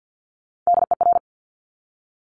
Morse code, heard as a keyed tone, about 50 wpm, 700 Hz. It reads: DER
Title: Top Three Letter Combinations in Words 50wpm